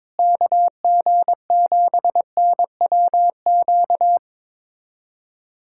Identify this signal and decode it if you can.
Morse code, heard as a keyed tone, about 22 wpm, 700 Hz. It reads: KG7NWQ
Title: USA Callsign DE Callsign 22wpm